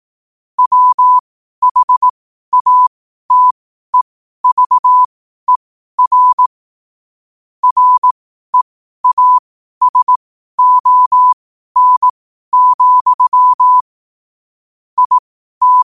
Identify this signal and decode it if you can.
Morse code, heard as a keyed tone, about 18 wpm, spaced as for 12 wpm, 1000 Hz. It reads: WHATEVER REASON, IT
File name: qotdfiles / 12wpm